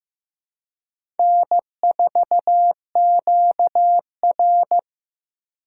Morse code, heard as a keyed tone, about 15 wpm, 700 Hz. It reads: N4QR
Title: USA Call Signs - 4 Characters - 4x Character Spacing - 15wpm